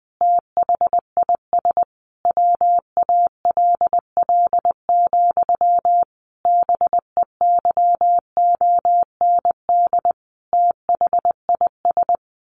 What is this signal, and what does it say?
Morse code, heard as a keyed tone, about 20 wpm, 700 Hz. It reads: THIS WALL, BEYOND THIS
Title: A Sound Of Thunder 20wpm